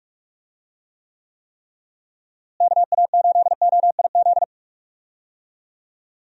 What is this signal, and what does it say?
Morse code, heard as a keyed tone, about 46 wpm, 700 Hz. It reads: KA8OIZ
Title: USA Call Signs - 6 Characters - Repeated 3x - 45wpm